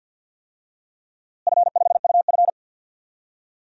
Morse code, heard as a keyed tone, about 53 wpm, 700 Hz. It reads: F5UF
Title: DXCC-100 Call Signs - 4 Characters - Repeated 3x - 50wpm